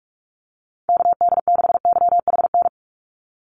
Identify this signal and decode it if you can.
Morse code, heard as a keyed tone, about 44 wpm, 700 Hz. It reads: KD6YHN
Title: USA Call Signs 45wpm